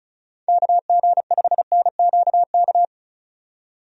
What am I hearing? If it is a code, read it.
Morse code, heard as a keyed tone, about 34 wpm, 700 Hz. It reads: KG5NQK
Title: USA Call Signs - 6 Characters - 35wpm